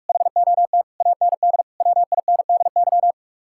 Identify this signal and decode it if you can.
Morse code, heard as a keyed tone, about 54 wpm, 700 Hz. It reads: HOT AND WINDY
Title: Speed-Racing Common QSO Phrases - 45wpm